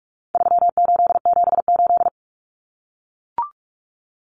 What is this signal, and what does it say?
Morse code, heard as a keyed tone, about 44 wpm, 700 Hz. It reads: 3878
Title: Numbers - 4-Digit 45wpm